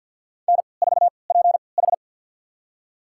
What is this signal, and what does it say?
Morse code, heard as a keyed tone, about 51 wpm, spaced as for 28 wpm, 700 Hz. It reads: N4PH
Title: USA Call Signs - 4 Characters - 3x Character Spacing - 50wpm